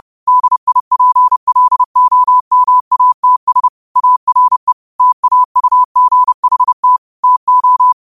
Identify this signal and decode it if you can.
Morse code, heard as a keyed tone, about 30 wpm, 1000 Hz. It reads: DIPLOMATS ARE TAUGHT TO
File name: 30wpm